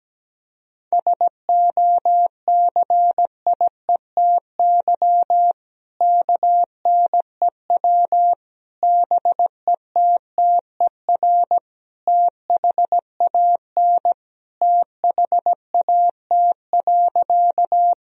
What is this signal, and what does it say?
Morse code, heard as a keyed tone, about 17 wpm, 700 Hz. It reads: SOCIETY KNEW BETTER THAN THAT.
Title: Sentences from Top 1000 Words - Review Part 4 - 17wpm